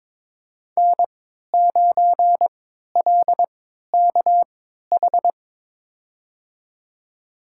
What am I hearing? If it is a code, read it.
Morse code, heard as a keyed tone, about 22 wpm, spaced as for 12 wpm, 700 Hz. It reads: N9LKH
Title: USA Call Signs - 5 Characters - 3x Character Spacing - 22wpm